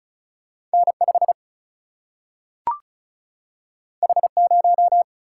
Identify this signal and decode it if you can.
Morse code, heard as a keyed tone, about 36 wpm, 700 Hz. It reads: N 5 H 0
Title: Single-Letter Single-Number 35wpm